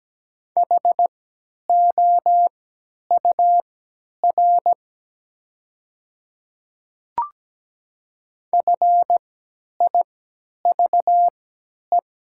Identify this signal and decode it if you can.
Morse code, heard as a keyed tone, about 17 wpm, spaced as for 10 wpm, 700 Hz. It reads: HOUR FIVE
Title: 3x Character Spacing - Top 500 Words - 17wpm